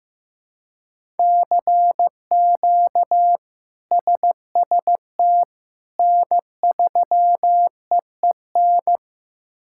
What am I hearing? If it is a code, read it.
Morse code, heard as a keyed tone, about 15 wpm, 700 Hz. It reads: CQ SST N3EEN
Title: K1USN SST QSO Practice - 15wpm